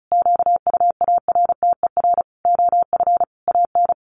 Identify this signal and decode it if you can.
Morse code, heard as a keyed tone, about 34 wpm, 700 Hz. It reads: QUARTER OF AN
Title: Sentences from Top 1300 Words 35wpm